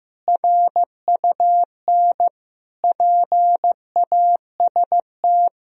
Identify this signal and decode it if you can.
Morse code, heard as a keyed tone, about 15 wpm, 700 Hz. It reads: RUN PAST
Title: Sentences from Top 1000 Words - Review Part 3 - 15wpm